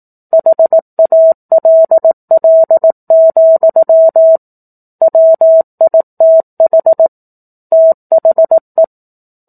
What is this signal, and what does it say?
Morse code, heard as a keyed tone, about 18 wpm, 650 Hz. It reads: HALL, WITH THE
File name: 04-Chapter-04 18 WPM